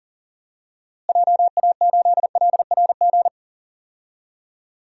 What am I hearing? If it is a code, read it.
Morse code, heard as a keyed tone, about 40 wpm, 700 Hz. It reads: JA8LRG
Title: DXCC-100 Call Signs - 6 Characters - Repeated 3x - 40wpm